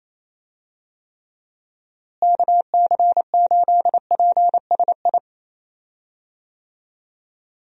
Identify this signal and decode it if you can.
Morse code, heard as a keyed tone, about 28 wpm, 700 Hz. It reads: KC8PSI
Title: USA Call Signs - 6 Characters - Repeated 3x - 28wpm